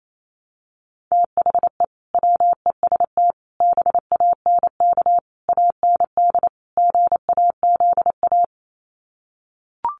Morse code, heard as a keyed tone, about 28 wpm, 700 Hz. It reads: THE WEST BANK AND GAZA
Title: Sets of 5 Words 28wpm